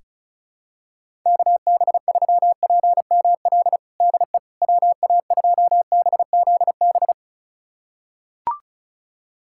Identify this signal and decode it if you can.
Morse code, heard as a keyed tone, about 36 wpm, 700 Hz. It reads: KB3PML DE WA2BZB
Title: USA Callsign DE Callsign - 6 Characters - 35wpm